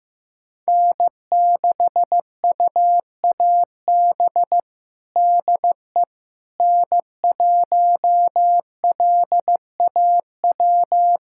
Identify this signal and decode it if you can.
Morse code, heard as a keyed tone, about 15 wpm, 700 Hz. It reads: N6UAB DE N1LAW
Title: USA Callsign DE Callsign - 5 Characters - 15wpm